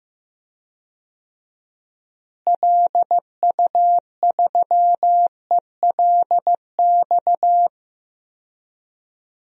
Morse code, heard as a keyed tone, about 15 wpm, 700 Hz. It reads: LU3ELX
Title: DXCC-100 Call Signs - 6 Characters - Repeated 3x - 15wpm